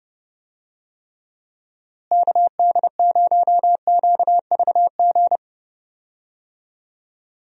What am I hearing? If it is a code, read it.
Morse code, heard as a keyed tone, about 30 wpm, 700 Hz. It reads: KD0QVG
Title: USA Call Signs - 6 Characters - Repeated 3x - 30wpm